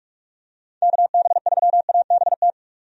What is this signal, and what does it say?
Morse code, heard as a keyed tone, about 43 wpm, 700 Hz. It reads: KB3ABT
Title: USA Call Signs - 6 Characters - 45wpm